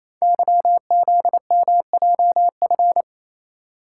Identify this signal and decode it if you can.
Morse code, heard as a keyed tone, about 29 wpm, 700 Hz. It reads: YZMJF
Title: Code Groups - 5 Letters - 28wpm